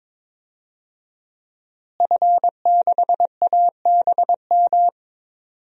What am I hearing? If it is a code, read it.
Morse code, heard as a keyed tone, about 22 wpm, 700 Hz. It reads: F6ABM
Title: DXCC-100 Call Signs - 5 Characters - Repeated 3x - 22wpm